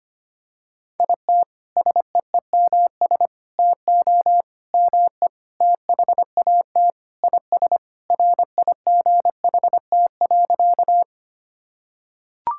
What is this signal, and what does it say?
Morse code, heard as a keyed tone, about 25 wpm, 700 Hz. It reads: IT SEEMS TO ME THAT IS RIGHT.
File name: practice / 25wpm / Sentences with 4-Word Phrases - 25wpm